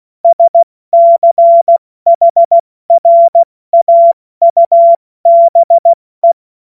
Morse code, heard as a keyed tone, about 16 wpm, 680 Hz. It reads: SCHRAUBE
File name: Morsekurs / Witze / Witz 07. September 2021, 70 F80